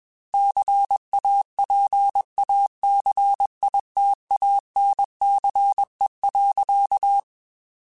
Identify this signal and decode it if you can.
Morse code, heard as a keyed tone, about 21 wpm, 800 Hz. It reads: CAPACITANCE.
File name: Technical Subjects / Technical Subjects-mp3s / XC @20